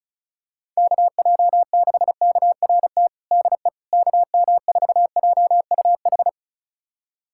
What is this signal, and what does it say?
Morse code, heard as a keyed tone, about 34 wpm, 700 Hz. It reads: KJ6KRT DE KM4JUH
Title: USA Callsign DE Callsign 35wpm